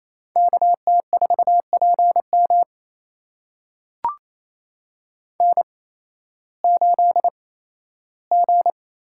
Morse code, heard as a keyed tone, about 28 wpm, spaced as for 15 wpm, 700 Hz. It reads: KT4PM N8G
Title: USA Call Signs - 5 Characters - 8x Character Spacing - 28wpm